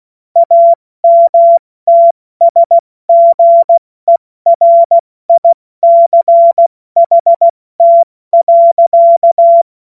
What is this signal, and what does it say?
Morse code, heard as a keyed tone, about 16 wpm, 680 Hz. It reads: AMTSGERICHT.
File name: Morsekurs / Witze / Witz 14.November 2023 70 F80